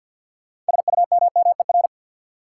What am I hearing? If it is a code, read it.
Morse code, heard as a keyed tone, about 52 wpm, 700 Hz. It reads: SUMMER
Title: Sentences from Top 1000 Words - Review Part 6 - New Word Drill - 50wpm